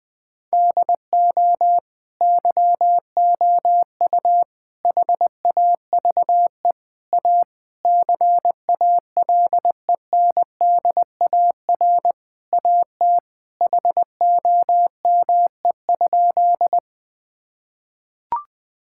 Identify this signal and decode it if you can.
Morse code, heard as a keyed tone, about 20 wpm, 700 Hz. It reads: DO YOU HAVE A CALENDAR AT HOME?
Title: Sentences from Top 1600 Words 20wpm